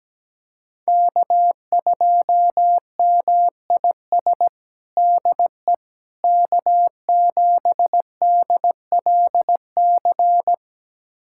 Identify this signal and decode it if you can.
Morse code, heard as a keyed tone, about 17 wpm, 700 Hz. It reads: K2MIS DE K7DLC
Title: USA Callsign DE Callsign - 5 Characters - 17wpm